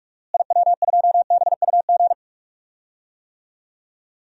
Morse code, heard as a keyed tone, about 43 wpm, 700 Hz. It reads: IW2BUG
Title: DXCC-100 Call Signs - 6 Characters - Repeated 2x - 45wpm